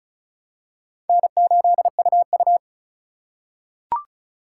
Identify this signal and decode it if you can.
Morse code, heard as a keyed tone, about 36 wpm, 700 Hz. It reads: N8UU
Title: USA Call Signs - 4 Characters - 3x Character Spacing - 35wpm